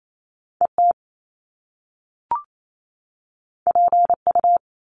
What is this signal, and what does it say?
Morse code, heard as a keyed tone, about 28 wpm, 700 Hz. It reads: ET PU
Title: Top Two Letter Combinations in Words 28wpm